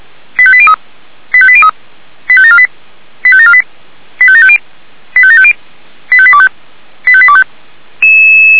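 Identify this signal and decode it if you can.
Five-tone selective call (ZVEI), heard as quick runs of five tones, tones of about 70 ms. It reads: 85803 85803 85748 85748 85750 85750 85825 85825
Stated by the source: ZVEI five-tone call 85726 received by SDR.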